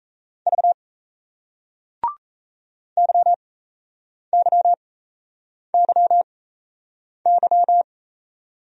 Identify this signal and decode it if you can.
Morse code, heard as a keyed tone, about 38 wpm, spaced as for 8 wpm, 700 Hz. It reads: V YYYY